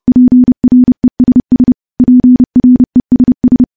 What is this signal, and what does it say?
Morse code, heard as a keyed tone, about 30 wpm, 250 Hz. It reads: PRESS PRESS
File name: morse / morse-training-30WPM